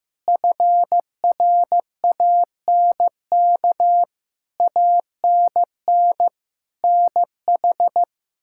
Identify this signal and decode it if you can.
Morse code, heard as a keyed tone, about 15 wpm, 700 Hz. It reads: FRANK ANN NH
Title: K1USN SST QSO Practice - 15wpm